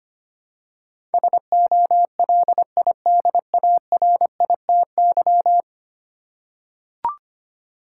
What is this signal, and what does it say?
Morse code, heard as a keyed tone, about 25 wpm, 700 Hz. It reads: SOLIDARITY
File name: 10 Letter Words - First 3 Letters Given - 25wpm